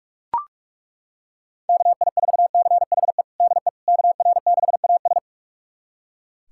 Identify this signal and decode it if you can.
Morse code, heard as a keyed tone, about 41 wpm, 700 Hz. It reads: KI4CHE DE KR6AS
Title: USA Callsign DE Callsign 45wpm